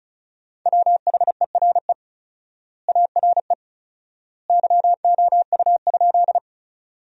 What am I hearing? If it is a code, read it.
Morse code, heard as a keyed tone, about 36 wpm, 700 Hz. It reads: WHERE ARE YOU?
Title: 4x Word-Spacing - Sentences from Top 200 Words - 35wpm